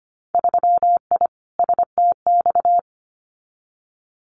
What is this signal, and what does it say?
Morse code, heard as a keyed tone, about 25 wpm, 700 Hz. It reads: 3I STX
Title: Winter Field Day Exchanges Encore 25wpm